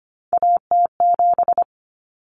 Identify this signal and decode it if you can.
Morse code, heard as a keyed tone, about 25 wpm, 700 Hz. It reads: AT7
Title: Common Cut Numbers - 3-Digit 25wpm